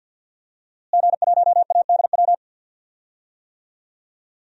Morse code, heard as a keyed tone, about 52 wpm, 700 Hz. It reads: G1ADW
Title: DXCC-100 Call Signs - 5 Characters - Repeated 2x - 50wpm